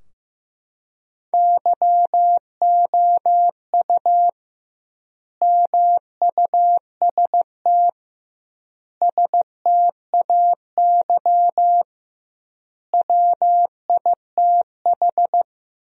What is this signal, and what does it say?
Morse code, heard as a keyed tone, about 15 wpm, 700 Hz. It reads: YOU MUST STAY WITH
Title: 2x Word-Spacing - Sentences from Top 500 Words - 15wpm